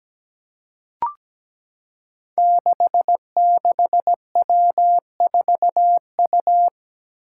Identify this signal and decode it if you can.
Morse code, heard as a keyed tone, about 17 wpm, 700 Hz. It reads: 66W4U
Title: Five Challenging Letter-Number Combinations 17wpm